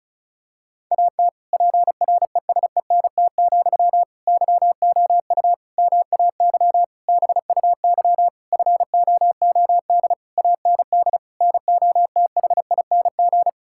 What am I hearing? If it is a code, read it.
Morse code, heard as a keyed tone, about 36 wpm, 700 Hz. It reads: AT PRESENT, YOU MAY BUY FOOD AND NOTHING
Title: Sentences from Top 1000 Words - Review Part 5 - 35wpm